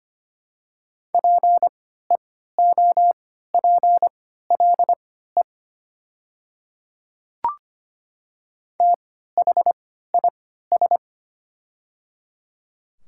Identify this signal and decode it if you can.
Morse code, heard as a keyed tone, about 25 wpm, spaced as for 14 wpm, 700 Hz. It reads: PEOPLE THIS
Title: 3x Character Spacing - Top 100 Words - 25wpm